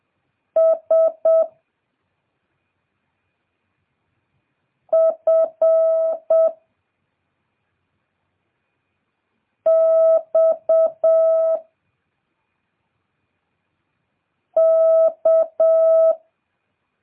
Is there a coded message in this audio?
Yes: Morse code from 0.6 to 16.1 s, heard as a keyed tone, about 7 wpm, 650 Hz. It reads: S F X K